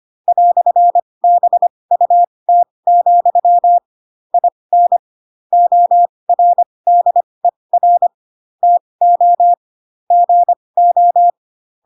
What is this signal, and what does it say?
Morse code, heard as a keyed tone, about 25 wpm, spaced as for 20 wpm, 700 Hz. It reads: "BUT, IN ORDER TO GO
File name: Pinocchio / Pinocchio ch8 15 25 ik1hge